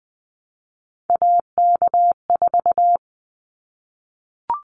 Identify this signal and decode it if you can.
Morse code, heard as a keyed tone, about 20 wpm, 700 Hz. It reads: AK4
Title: Callsign Prefixes for Top 25 DXCC - 20wpm